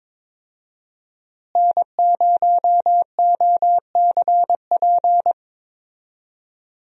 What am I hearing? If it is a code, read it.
Morse code, heard as a keyed tone, about 21 wpm, 700 Hz. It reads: N0OCP
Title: USA Call Signs - 5 Characters - Repeated 3x - 22wpm